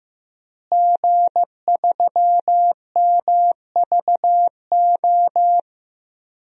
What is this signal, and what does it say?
Morse code, heard as a keyed tone, about 15 wpm, 700 Hz. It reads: G3MVO
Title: DXCC-100 Call Signs - 5 Characters - Repeated 2x - 15wpm